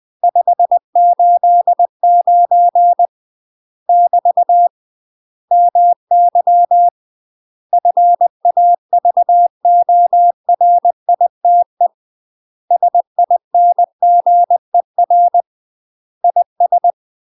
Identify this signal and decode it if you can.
Morse code, heard as a keyed tone, about 20 wpm, 700 Hz. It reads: 589 <BT> MY FAVORITE SINGER IS
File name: UK MASTER QSOs TRAINING / 20WPM / UK MASTER QSOs 0060